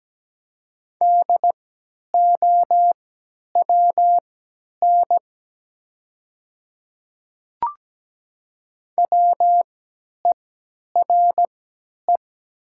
Morse code, heard as a keyed tone, about 17 wpm, spaced as for 10 wpm, 700 Hz. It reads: DOWN WERE
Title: 3x Character Spacing - Top 100 Words - 17wpm